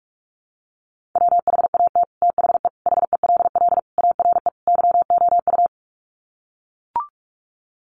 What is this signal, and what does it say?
Morse code, heard as a keyed tone, about 44 wpm, 700 Hz. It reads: WHAT THE HELL ARE YOU
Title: Sets of 5 Words Encore 45wpm